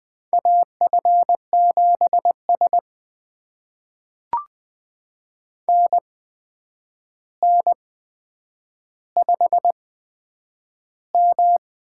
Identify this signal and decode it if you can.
Morse code, heard as a keyed tone, about 20 wpm, spaced as for 9 wpm, 700 Hz. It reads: AF7S NN5M
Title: USA Call Signs - 4 Characters - 8x Character Spacing - 20wpm